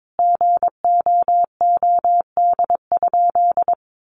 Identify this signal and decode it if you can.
Morse code, heard as a keyed tone, about 22 wpm, 700 Hz. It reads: GOOD?
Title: Sentences from Top 1300 Words 22wpm